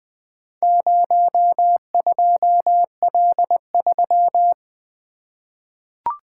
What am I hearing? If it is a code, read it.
Morse code, heard as a keyed tone, about 20 wpm, 700 Hz. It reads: 02L3
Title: Four Letter-Number Combinations - 20wpm